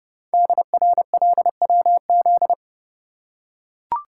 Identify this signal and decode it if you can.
Morse code, heard as a keyed tone, about 30 wpm, 700 Hz. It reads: DRLWZ
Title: Code Groups - 5 Letters - 30wpm